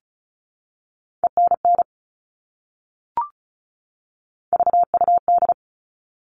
Morse code, heard as a keyed tone, about 34 wpm, 700 Hz. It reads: ENN VUD
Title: All Cut Numbers - 3-Digit 35wpm